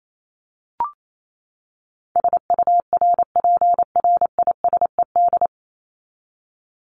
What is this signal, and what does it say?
Morse code, heard as a keyed tone, about 28 wpm, 700 Hz. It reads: SURPRISED
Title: Top 1501-1750 Words 28wpm